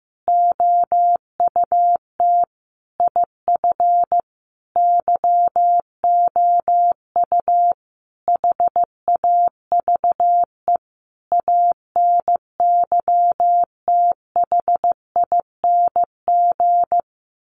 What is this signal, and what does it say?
Morse code, heard as a keyed tone, about 15 wpm, 700 Hz. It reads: OUT IF YOU HAVE ANYTHING
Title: Sentences from Top 1300 Words 15wpm